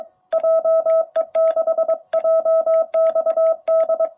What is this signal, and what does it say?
Morse code, heard as a keyed tone, about 21 wpm, 660 Hz. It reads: JE6JXD
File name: jh8cbh / 2021cw